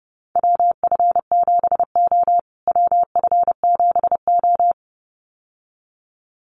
Morse code, heard as a keyed tone, about 30 wpm, 700 Hz. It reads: WF7O WF7O
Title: USA Call Signs Repeated 2x 30wpm